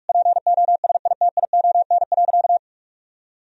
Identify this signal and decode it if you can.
Morse code, heard as a keyed tone, about 46 wpm, 700 Hz. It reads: POSITION.
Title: Sentences from Top 1000 Words - Review Part 3 - 45wpm